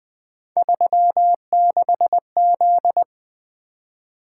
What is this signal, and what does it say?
Morse code, heard as a keyed tone, about 24 wpm, 700 Hz. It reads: 36Z
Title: Speed-Racing - Three Letter-Number Combinations - 20wpm